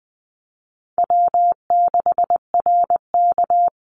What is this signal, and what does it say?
Morse code, heard as a keyed tone, about 20 wpm, 700 Hz. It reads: W6RK
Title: Call Signs Repeated 3x - 20wpm